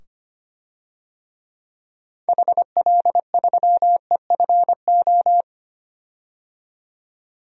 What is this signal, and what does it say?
Morse code, heard as a keyed tone, about 25 wpm, 700 Hz. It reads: HL3EFO
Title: DXCC-100 Call Signs - 6 Characters - 25wpm